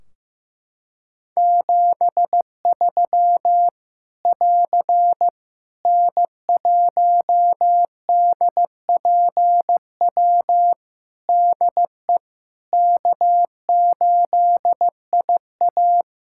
Morse code, heard as a keyed tone, about 15 wpm, 700 Hz. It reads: 73 <AR> N1DPW DE K8IA
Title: K1USN SST QSO Practice 15wpm